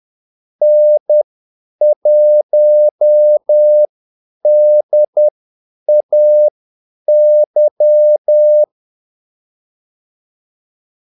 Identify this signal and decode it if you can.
Morse code, heard as a keyed tone, about 10 wpm, spaced as for 8 wpm, 600 Hz. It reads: N1DAY
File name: Callsigns-10-8wpm